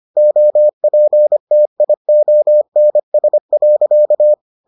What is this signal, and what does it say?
Morse code, heard as a keyed tone, about 25 wpm, 600 Hz. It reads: OPTIONS.